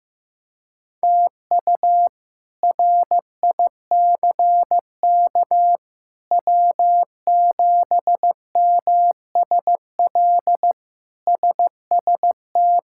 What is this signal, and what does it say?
Morse code, heard as a keyed tone, about 16 wpm, 700 Hz. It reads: TU RICK W7MSL SST
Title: K1USN SST QSO Practice - 15wpm